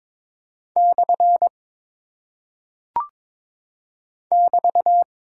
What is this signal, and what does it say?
Morse code, heard as a keyed tone, about 22 wpm, 700 Hz. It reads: / <BT>
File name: practice / 22wpm / Single Punctuation-Prosign - 22wpm